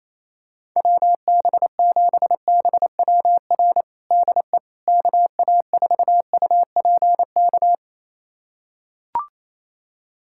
Morse code, heard as a keyed tone, about 28 wpm, 700 Hz. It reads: WB7BWR DE KA4UPK
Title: USA Callsign DE Callsign 28wpm